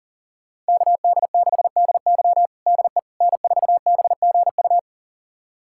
Morse code, heard as a keyed tone, about 40 wpm, 700 Hz. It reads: KD6DY DE N4BGU
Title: USA Callsign DE Callsign - 5 Characters - 40wpm